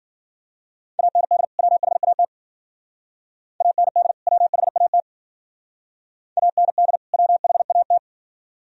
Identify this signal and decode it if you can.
Morse code, heard as a keyed tone, about 52 wpm, 700 Hz. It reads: AND WHAT AND WHAT AND WHAT